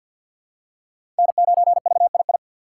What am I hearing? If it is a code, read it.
Morse code, heard as a keyed tone, about 51 wpm, 700 Hz. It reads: N9VII
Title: USA Call Signs - 5 Characters - 4x Character Spacing - 50wpm